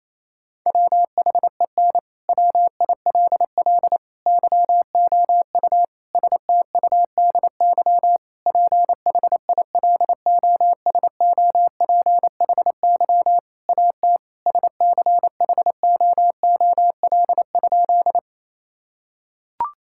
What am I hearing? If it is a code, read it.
Morse code, heard as a keyed tone, about 27 wpm, 700 Hz. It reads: WHEN WILL YOU STUDY PHILOSOPHY AT SCHOOL?
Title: Sentences from Top 1000 Words - Review Part 6 - 28wpm